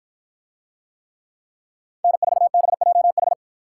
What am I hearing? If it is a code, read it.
Morse code, heard as a keyed tone, about 53 wpm, 700 Hz. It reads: N4BJH